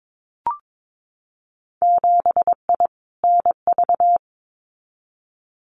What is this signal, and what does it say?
Morse code, heard as a keyed tone, about 22 wpm, 700 Hz. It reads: 7I NV
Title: Winter Field Day Exchanges Encore 22wpm